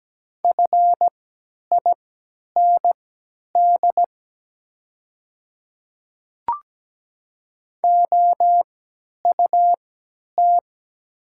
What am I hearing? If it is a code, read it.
Morse code, heard as a keyed tone, about 17 wpm, spaced as for 10 wpm, 690 Hz. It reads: FIND OUT
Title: 3x Character Spacing - Top 100 Words - 17wpm